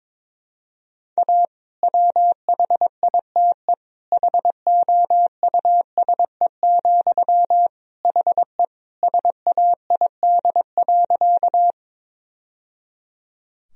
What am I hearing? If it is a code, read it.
Morse code, heard as a keyed tone, about 21 wpm, 700 Hz. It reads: A WHITE HOUSE, HE SAID.
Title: Sentences from Top 1000 Words - Review Part 1 - 22wpm